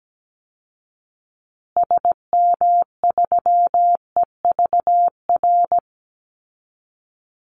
Morse code, heard as a keyed tone, about 17 wpm, 700 Hz. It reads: SM3EVR
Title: Call Signs Repeated 2x - 17wpm